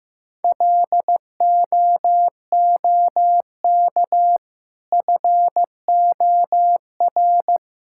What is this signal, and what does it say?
Morse code, heard as a keyed tone, about 15 wpm, 700 Hz. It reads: LOOK FOR
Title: Sentences from Top 1000 Words - Review Part 6 - 15wpm